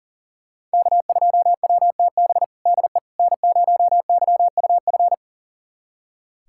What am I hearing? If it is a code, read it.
Morse code, heard as a keyed tone, about 40 wpm, 700 Hz. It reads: K2WTB DE N0YUF
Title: USA Callsign DE Callsign - 5 Characters - 40wpm